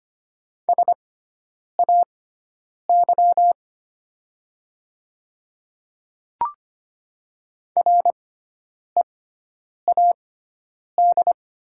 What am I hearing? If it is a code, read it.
Morse code, heard as a keyed tone, about 25 wpm, spaced as for 9 wpm, 700 Hz. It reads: SAY READ
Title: 6x Character Spacing - Top 200 Words - 25wpm